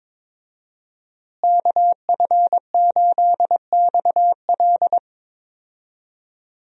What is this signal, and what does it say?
Morse code, heard as a keyed tone, about 22 wpm, 700 Hz. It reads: KF8XL
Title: USA Call Signs - 5 Characters - Repeated 2x - 22wpm